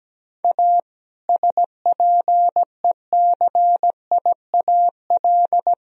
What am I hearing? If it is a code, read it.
Morse code, heard as a keyed tone, about 17 wpm, 700 Hz. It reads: A SPECIAL
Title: Sentences from Top 1000 Words - Review Part 3 - 17wpm